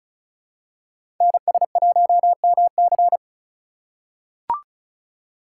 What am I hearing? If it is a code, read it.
Morse code, heard as a keyed tone, about 36 wpm, 700 Hz. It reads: NS1MC
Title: USA Call Signs - 5 Characters - 4x Character Spacing - 35wpm